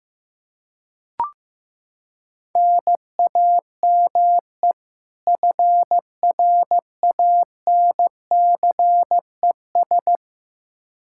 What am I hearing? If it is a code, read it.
Morse code, heard as a keyed tone, about 15 wpm, 700 Hz. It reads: NAME FRANCES
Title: Practice QSO - Exchange 1 - Beginning - 15wpm